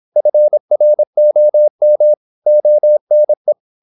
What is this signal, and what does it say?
Morse code, heard as a keyed tone, about 26 wpm, 600 Hz. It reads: FROM ONE
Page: Never 26wpm | CW med Gnister